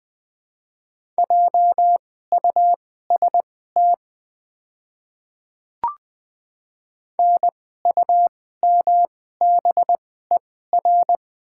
Morse code, heard as a keyed tone, about 20 wpm, spaced as for 14 wpm, 700 Hz. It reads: JUST NUMBER
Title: 2x Character Spacing - Top 200 Words - 20wpm